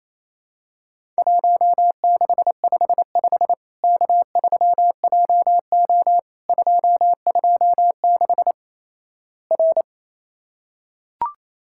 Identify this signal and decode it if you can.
Morse code, heard as a keyed tone, about 28 wpm, 700 Hz. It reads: 1655 K3JO 226 R
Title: Worked All Europe DX Contest - QTC Exchange - 28wpm